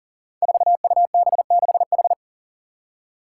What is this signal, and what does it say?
Morse code, heard as a keyed tone, about 40 wpm, 700 Hz. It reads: 4UB6H
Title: Five Challenging Letter-Number Combinations 40wpm